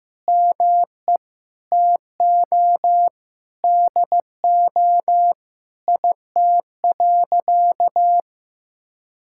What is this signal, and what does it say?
Morse code, heard as a keyed tone, about 15 wpm, 700 Hz. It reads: ME TO DO IT.
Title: Sentences from Top 1000 Words - Review Part 5 - 15wpm